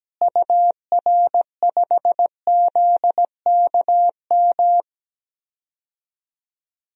Morse code, heard as a keyed tone, about 17 wpm, 700 Hz. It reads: UR5ZKM
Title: DXCC-100 Call Signs - 6 Characters - Repeated 3x - 17wpm